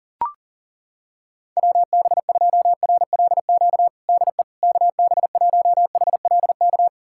Morse code, heard as a keyed tone, about 40 wpm, 700 Hz. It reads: WB2RLQ DE KB1HLK
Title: USA Callsign DE Callsign - 6 Characters - 40wpm